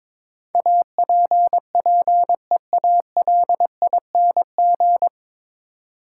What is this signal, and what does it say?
Morse code, heard as a keyed tone, about 22 wpm, 700 Hz. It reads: APPEALING
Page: Morse Code Ninja practice 22wpm